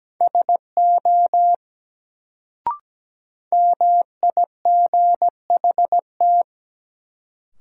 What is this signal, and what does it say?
Morse code, heard as a keyed tone, about 17 wpm, 700 Hz. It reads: SO MIGHT
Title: Sets of 2 Words - New Word Drill - 17wpm